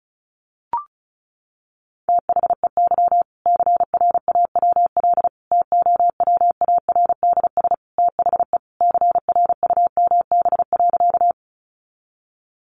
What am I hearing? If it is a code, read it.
Morse code, heard as a keyed tone, about 34 wpm, 700 Hz. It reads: THEY CRAWL TOWARDS THE CRUMB.
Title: Additional Easy Phrases 35wpm